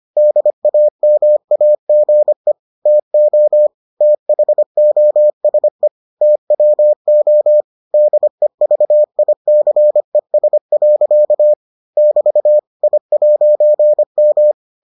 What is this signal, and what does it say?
Morse code, heard as a keyed tone, about 25 wpm, 600 Hz. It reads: DAMAGE TO THOSE TWO DEVICES. <BT> I'M